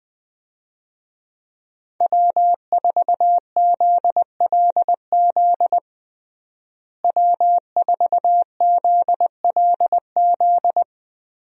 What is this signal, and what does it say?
Morse code, heard as a keyed tone, about 20 wpm, 700 Hz. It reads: W4ZLZ W4ZLZ
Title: USA Call Signs - 5 Characters - Repeated 3x - 20wpm